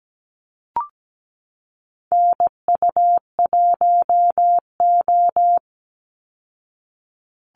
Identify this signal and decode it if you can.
Morse code, heard as a keyed tone, about 17 wpm, 700 Hz. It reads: NU1O
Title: Call Signs Repeated 2x - 17wpm